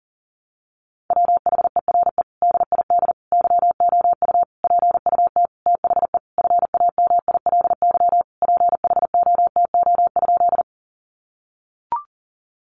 Morse code, heard as a keyed tone, about 40 wpm, 700 Hz. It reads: WHERE DID YOU PUT THE FAMILY PHOTO?
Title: Sentences from Top 1200 Words 40wpm